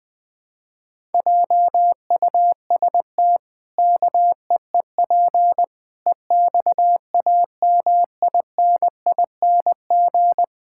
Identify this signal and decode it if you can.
Morse code, heard as a keyed tone, about 19 wpm, 700 Hz. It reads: JUST KEEP EXAMINING
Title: Pangram Challenge v2 - 20wpm